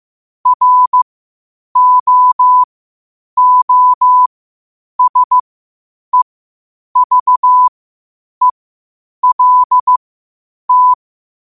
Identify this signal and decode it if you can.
Morse code, heard as a keyed tone, about 15 wpm, spaced as for 8 wpm, 1000 Hz. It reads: ROOSEVELT
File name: qotdfiles / 8wpm